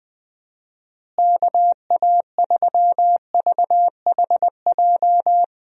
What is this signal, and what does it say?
Morse code, heard as a keyed tone, about 20 wpm, 700 Hz. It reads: KA3VHJ
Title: USA Call Signs - 6 Characters - Repeated 2x - 20wpm